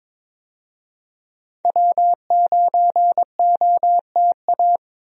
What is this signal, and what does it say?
Morse code, heard as a keyed tone, about 22 wpm, 700 Hz. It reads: W9OTA
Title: USA Call Signs - 5 Characters - 22wpm